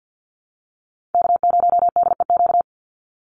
Morse code, heard as a keyed tone, about 50 wpm, 700 Hz. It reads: K0DEQ
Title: Call Signs Repeated 2x - 50wpm